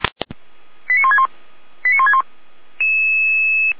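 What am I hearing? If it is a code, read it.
Five-tone selective call (ZVEI), heard as quick runs of five tones, tones of about 70 ms. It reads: 89161 89161
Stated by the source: ZVEI five-tone call 89243 received by SDR.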